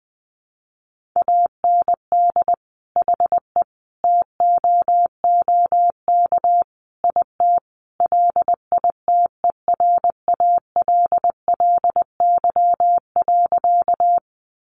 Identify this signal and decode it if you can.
Morse code, heard as a keyed tone, about 20 wpm, 700 Hz. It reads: AND HE TOOK IT LITERALLY.
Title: Sentences from Top 1400 Words 20wpm